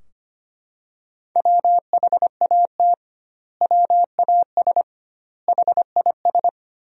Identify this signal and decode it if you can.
Morse code, heard as a keyed tone, about 25 wpm, 700 Hz. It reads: WHAT WAS HIS
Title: 2x Word-Spacing - Sentences from Top 200 Words - 25wpm